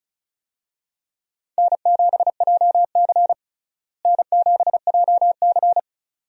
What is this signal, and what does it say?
Morse code, heard as a keyed tone, about 36 wpm, 700 Hz. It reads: N7JC N7JC
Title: USA Call Signs - 4 Characters - Repeated 2x - 35wpm